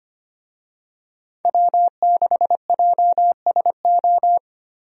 Morse code, heard as a keyed tone, about 25 wpm, 700 Hz. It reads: W6JSO
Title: USA Call Signs - 5 Characters - Repeated 3x - 25wpm